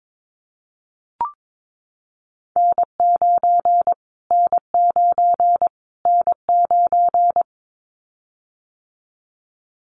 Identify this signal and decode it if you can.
Morse code, heard as a keyed tone, about 22 wpm, 700 Hz. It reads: N9 N9 N9
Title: Callsign Prefixes for Top 25 DXCC - Repeated 3x - 22wpm